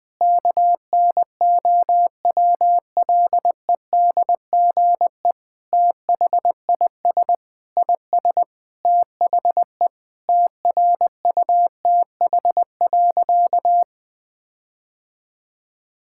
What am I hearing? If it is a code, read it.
Morse code, heard as a keyed tone, about 20 wpm, 700 Hz. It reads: KNOWLEDGE THIS IS THE TRUTH.
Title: Sentences with 9-Word Phrases - 20wpm